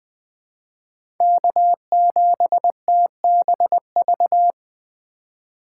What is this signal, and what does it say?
Morse code, heard as a keyed tone, about 20 wpm, 700 Hz. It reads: K7TBV
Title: USA Call Signs - 5 Characters - 2x Character Spacing - 20wpm